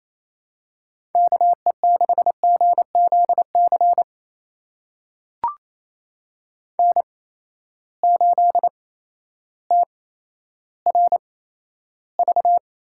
Morse code, heard as a keyed tone, about 27 wpm, spaced as for 13 wpm, 700 Hz. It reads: KE6GZC N8TRV
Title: USA Call Signs - 8x Character Spacing - 28wpm